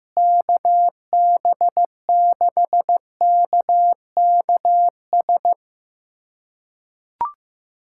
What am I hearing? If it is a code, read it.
Morse code, heard as a keyed tone, about 15 wpm, 700 Hz. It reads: KB6KKS
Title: USA Callsign DE Callsign - 6 Characters - 15wpm